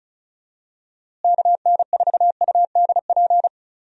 Morse code, heard as a keyed tone, about 36 wpm, 700 Hz. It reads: KN4UDP
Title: USA Call Signs - 6 Characters - Repeated 3x - 35wpm